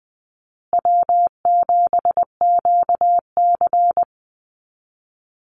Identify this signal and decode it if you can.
Morse code, heard as a keyed tone, about 19 wpm, 700 Hz. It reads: W7QC
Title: Call Signs Repeated 2x - 20wpm